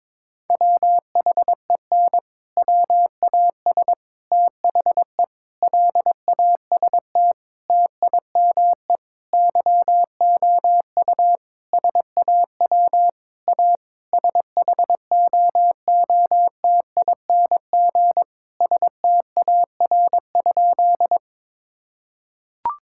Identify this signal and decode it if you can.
Morse code, heard as a keyed tone, about 22 wpm, 700 Hz. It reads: WHEN WAS THE LAST TIME YOU SAW A SHOOTING STAR?
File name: practice / 22wpm / Sentences with 8-Word Phrases - 22wpm